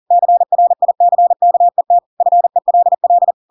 Morse code, heard as a keyed tone, about 40 wpm, 700 Hz. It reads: CRICKET FELL
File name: Pinocchio / Pinocchio ch4 40 40 ik1hge